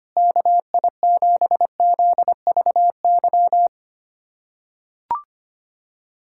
Morse code, heard as a keyed tone, about 25 wpm, 700 Hz. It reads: KI7ZVY
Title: USA Callsign DE Callsign - 6 Characters - 25wpm